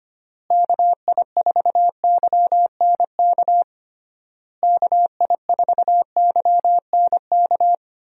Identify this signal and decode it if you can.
Morse code, heard as a keyed tone, about 25 wpm, 700 Hz. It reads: KI4YNK KI4YNK
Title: USA Call Signs - 6 Characters - Repeated 3x - 25wpm